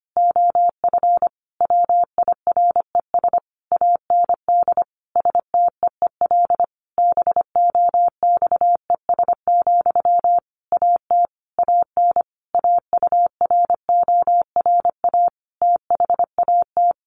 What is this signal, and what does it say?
Morse code, heard as a keyed tone, about 25 wpm, 700 Hz. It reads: OF WIRES AND STEEL BOXES, AT AN AURORA THAT
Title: A Sound Of Thunder 25wpm